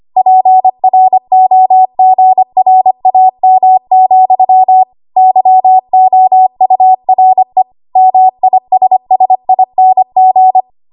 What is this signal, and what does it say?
Morse code, heard as a keyed tone, about 25 wpm, 750 Hz. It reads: PROGRAM, YOURE MISSING
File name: Archive / 25 WPM / 25WPM